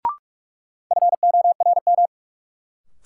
Morse code, heard as a keyed tone, about 46 wpm, 700 Hz. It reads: FORM
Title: Sentences from Top 1000 Words - Review Part 1 - New Word Drill - 45wpm